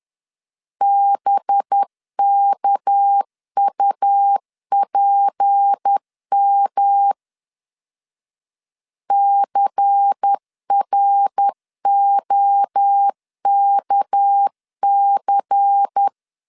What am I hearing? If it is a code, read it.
Morse code, heard as a keyed tone, about 10 wpm, 800 Hz. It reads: BKUPM CROKC
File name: LEZIONI / 6 PKX0 / 6 PK 10wpm 2436